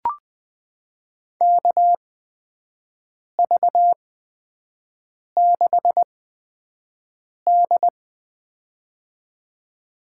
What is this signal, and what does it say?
Morse code, heard as a keyed tone, about 20 wpm, spaced as for 5 wpm, 700 Hz. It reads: KV6D
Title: USA Call Signs - 8x Character Spacing - 20wpm